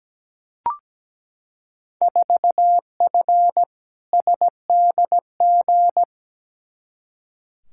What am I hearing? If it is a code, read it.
Morse code, heard as a keyed tone, about 17 wpm, 700 Hz. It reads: 4F SDG
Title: Field Day Exchanges - 17wpm